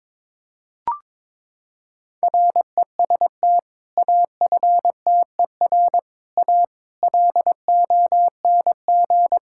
Morse code, heard as a keyed tone, about 22 wpm, 700 Hz. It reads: REST AFTER A LONG
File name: practice / 22wpm / Sentences with 2-Word Phrases - 22wpm